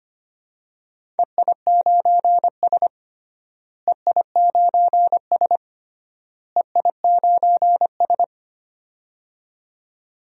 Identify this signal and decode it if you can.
Morse code, heard as a keyed tone, about 25 wpm, 700 Hz. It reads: EI9S EI9S EI9S
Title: DXCC-100 Call Signs - 4 Characters - Repeated 3x - 25wpm